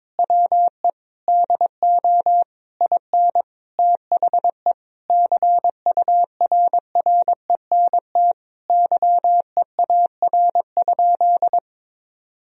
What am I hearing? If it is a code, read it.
Morse code, heard as a keyed tone, about 22 wpm, 700 Hz. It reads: WE DO IN THE CURRENT YEAR?
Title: Sentences from Top 1000 Words - Review Part 4 - 22wpm